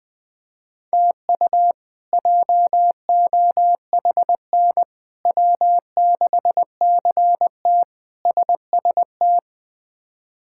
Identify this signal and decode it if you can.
Morse code, heard as a keyed tone, about 20 wpm, 700 Hz. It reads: TU JOHN W6CT SST
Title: K1USN SST QSO Practice - 20wpm